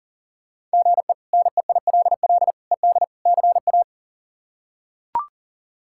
Morse code, heard as a keyed tone, about 37 wpm, 700 Hz. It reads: GE NEILL ED CA
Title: K1USN SST QSO Practice - 40wpm